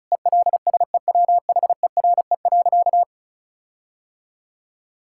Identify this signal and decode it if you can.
Morse code, heard as a keyed tone, about 34 wpm, 700 Hz. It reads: ELSEWHERE.
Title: Sentences from Top 1600 Words 35wpm